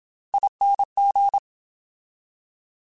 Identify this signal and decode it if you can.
Morse code, heard as a keyed tone, about 27 wpm, 800 Hz. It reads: ING
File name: Special INGs / Special INGs-mp3s / Slow ing @25